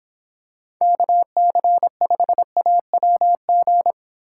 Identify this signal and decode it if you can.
Morse code, heard as a keyed tone, about 26 wpm, 700 Hz. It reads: KC5AWG